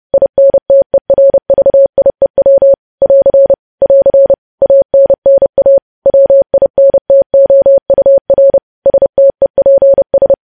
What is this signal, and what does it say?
Morse code, heard as a keyed tone, about 30 wpm, 550 Hz. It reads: INTERVIEW <AR> <AR> ANNA WINTOUR STEPS
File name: podcast / News headlines-1x-30x30